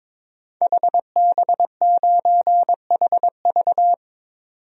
Morse code, heard as a keyed tone, about 21 wpm, 700 Hz. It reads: HB9HV
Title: DXCC-100 Call Signs - 5 Characters - 22wpm